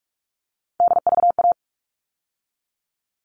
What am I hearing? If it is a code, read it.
Morse code, heard as a keyed tone, about 44 wpm, 700 Hz. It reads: DVA
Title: All Cut Numbers - 3-Digit 45wpm